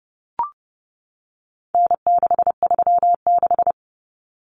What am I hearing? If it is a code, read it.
Morse code, heard as a keyed tone, about 30 wpm, 700 Hz. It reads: N636
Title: Common Cut Numbers - 4-Digit 30wpm